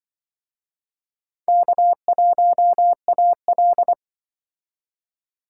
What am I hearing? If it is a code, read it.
Morse code, heard as a keyed tone, about 24 wpm, 700 Hz. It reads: K1AL
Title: Speed Racing - USA 1x2 Call Signs 20wpm